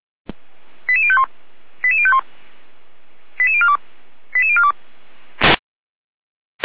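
Five-tone selective call (ZVEI), heard as quick runs of five tones, tones of about 70 ms. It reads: 88051 88051 88042 88042
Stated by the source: ZVEI five-tone call 89409 received by SDR.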